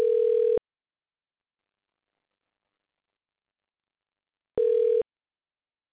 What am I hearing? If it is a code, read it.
Ringback tone (North America), 440 + 480 Hz.